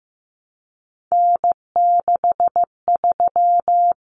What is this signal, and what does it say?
Morse code, heard as a keyed tone, about 15 wpm, 700 Hz. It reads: N63
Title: Common Cut Numbers - 3-Digit 15wpm